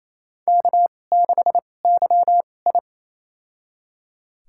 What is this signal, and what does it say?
Morse code, heard as a keyed tone, about 27 wpm, spaced as for 20 wpm, 700 Hz. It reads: K6YI
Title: USA Call Signs - 2x Character Spacing - 28wpm